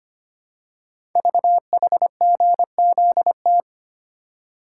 Morse code, heard as a keyed tone, about 25 wpm, 700 Hz. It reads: VHGZT
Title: Code Groups - 5 Letters - 25wpm